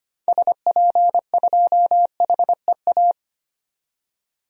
Morse code, heard as a keyed tone, about 25 wpm, 700 Hz. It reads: SP2HEA
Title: DXCC-100 Call Signs - 6 Characters - 25wpm